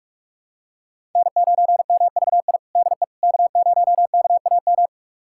Morse code, heard as a keyed tone, about 42 wpm, 700 Hz. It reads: N9MVI DE K0KAM
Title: USA Callsign DE Callsign - 5 Characters - 45wpm